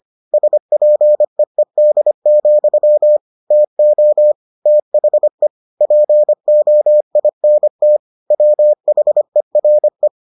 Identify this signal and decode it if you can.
Morse code, heard as a keyed tone, about 25 wpm, 600 Hz. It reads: SPEED, TO THE POINT WHERE